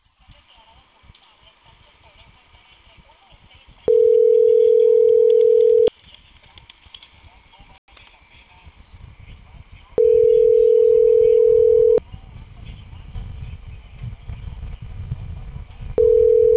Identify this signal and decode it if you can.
Ringback tone (North America), 440 + 480 Hz, with speech.
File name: CENTRAL CAEA / out agcaea02 - .35012